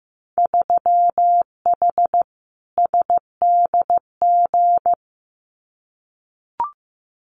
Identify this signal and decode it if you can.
Morse code, heard as a keyed tone, about 15 wpm, 700 Hz. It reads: 3H SDG
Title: Winter Field Day Exchanges Encore 15wpm